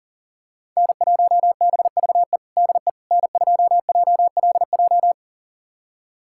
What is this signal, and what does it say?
Morse code, heard as a keyed tone, about 40 wpm, 700 Hz. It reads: N1BVE DE N2JLJ
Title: USA Callsign DE Callsign - 5 Characters - 40wpm